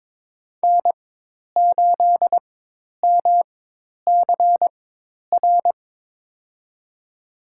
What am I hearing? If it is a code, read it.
Morse code, heard as a keyed tone, about 22 wpm, spaced as for 10 wpm, 700 Hz. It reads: N8MCR
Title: USA Call Signs - 4x Character Spacing - 22wpm